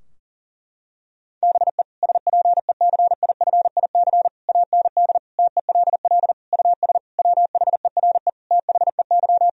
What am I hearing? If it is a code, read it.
Morse code, heard as a keyed tone, about 40 wpm, 700 Hz. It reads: BE SPECIFIC AND TELL US WHERE THEY
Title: Sentences from Top 1000 Words - Review Part 3 - 40wpm